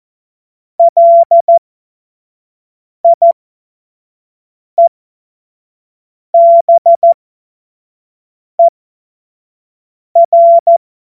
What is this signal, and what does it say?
Morse code, heard as a keyed tone, about 14 wpm, spaced as for 5 wpm, 680 Hz. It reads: LIEBER